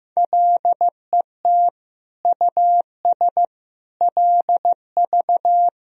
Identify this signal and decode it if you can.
Morse code, heard as a keyed tone, about 15 wpm, 700 Hz. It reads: LET US LV
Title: Sentences from Top 400 Words and Abbreviations 15wpm